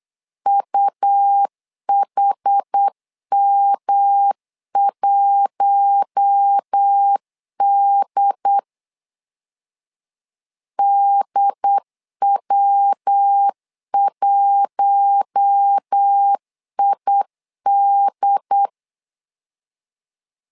Morse code, heard as a keyed tone, about 8 wpm, 800 Hz. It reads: UHM1D DW1ID
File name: LEZIONI / 4 Q1 8wpm 3035